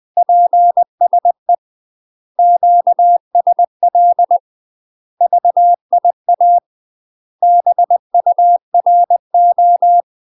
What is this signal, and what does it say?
Morse code, heard as a keyed tone, about 20 wpm, 700 Hz. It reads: PSE QSL VIA BURO
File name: UK MASTER QSOs TRAINING / 20WPM / UK MASTER QSOs 0003